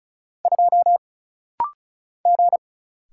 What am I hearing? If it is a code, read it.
Morse code, heard as a keyed tone, about 34 wpm, 700 Hz. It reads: 2 G
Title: Single Letter-Number - Rapid-Fire 35wpm